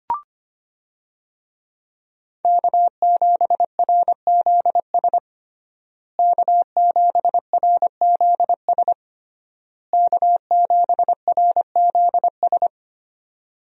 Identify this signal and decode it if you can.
Morse code, heard as a keyed tone, about 25 wpm, 700 Hz. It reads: K7RZS K7RZS K7RZS
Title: USA Call Signs - 5 Characters - Repeated 3x - 25wpm